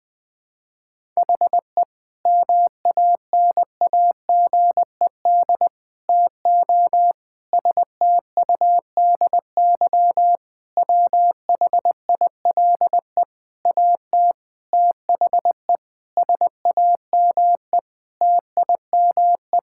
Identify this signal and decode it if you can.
Morse code, heard as a keyed tone, about 20 wpm, 700 Hz. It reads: HE MANAGED TO STUDY WHILE AT THE SAME TIME
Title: Sentences with 5-Word Phrases - 20wpm